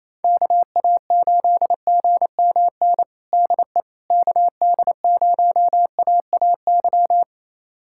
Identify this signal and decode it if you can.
Morse code, heard as a keyed tone, about 28 wpm, 700 Hz. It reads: KA8GMN DE KD0AAY
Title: USA Callsign DE Callsign 28wpm